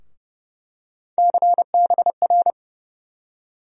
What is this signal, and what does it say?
Morse code, heard as a keyed tone, about 30 wpm, 700 Hz. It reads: CBR
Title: USA Call Sign Postfixes - 3 Letters 30wpm